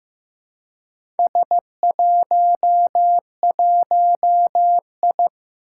Morse code, heard as a keyed tone, about 15 wpm, 700 Hz. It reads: S11I
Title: Four Challenging Letter-Number Combinations 15wpm